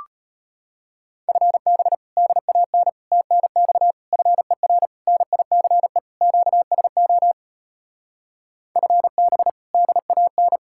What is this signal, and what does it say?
Morse code, heard as a keyed tone, about 33 wpm, 700 Hz. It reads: FB DAN TNX FER NICE QSO FB DAN